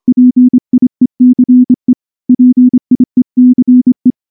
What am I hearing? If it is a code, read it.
Morse code, heard as a keyed tone, about 24 wpm, 250 Hz. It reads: PIECE PIECE
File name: morse / morse-training-25WPM